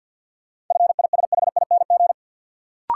Morse code, heard as a keyed tone, about 52 wpm, 700 Hz. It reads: FISHING
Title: Sentences from Top 1000 Words - Review Part 5 - New Word Drill - 50wpm